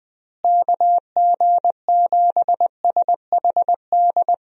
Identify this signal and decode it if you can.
Morse code, heard as a keyed tone, about 20 wpm, 700 Hz. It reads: KG7SHD
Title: USA Call Signs - 6 Characters - Repeated 2x - 20wpm